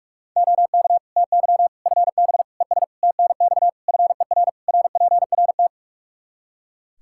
Morse code, heard as a keyed tone, about 41 wpm, 700 Hz. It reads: OK TY FB ES TNX FER RPRT
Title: K1USN SST QSO Practice 45wpm